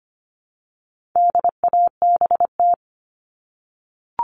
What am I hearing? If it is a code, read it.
Morse code, heard as a keyed tone, about 25 wpm, 700 Hz. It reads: DABT
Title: All Cut Numbers - 4-Digit 25wpm